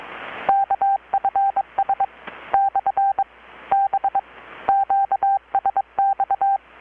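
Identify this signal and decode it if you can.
Morse code, heard as a keyed tone, about 23 wpm, 790 Hz. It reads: KFS / B QSX
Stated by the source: Morse from KPH's Night of Nights XX maritime radiotelegraphy event, received in Chiapas, Mexico about 3,800 km away on a Kenwood R-600 with a grounded 10 m long-wire antenna.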